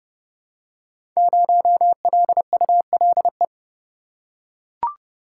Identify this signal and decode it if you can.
Morse code, heard as a keyed tone, about 30 wpm, 700 Hz. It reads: 0LULE
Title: Code Groups - 5 Letter:Numbers - 30wpm